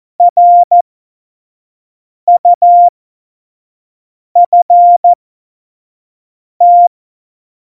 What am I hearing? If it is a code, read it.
Morse code, heard as a keyed tone, about 14 wpm, spaced as for 5 wpm, 700 Hz. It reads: RUFT
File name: Morsekurs / Witze / Witz 23. Oktober 2018, 25 F70